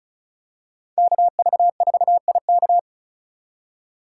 Morse code, heard as a keyed tone, about 36 wpm, 690 Hz. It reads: KV4IK
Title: USA Call Signs - 5 Characters - 3x Character Spacing - 35wpm